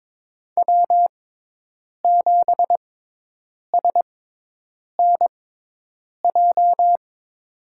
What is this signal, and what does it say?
Morse code, heard as a keyed tone, about 22 wpm, spaced as for 8 wpm, 700 Hz. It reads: W7SNJ
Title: USA Call Signs - 5 Characters - 6x Character Spacing - 22wpm